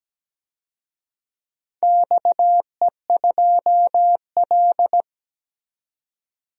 Morse code, heard as a keyed tone, about 17 wpm, 700 Hz. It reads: XE2L
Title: DXCC-100 Call Signs - 4 Characters - 17wpm